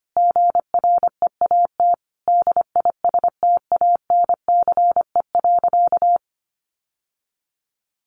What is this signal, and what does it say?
Morse code, heard as a keyed tone, about 25 wpm, 700 Hz. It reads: GREAT DISTANCE.
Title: Sentences from Top 1200 Words 25wpm